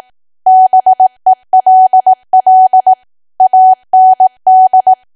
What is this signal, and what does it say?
Morse code, heard as a keyed tone, about 18 wpm, 750 Hz. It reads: BELL AND
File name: Archive / 18 WPM / 18WPM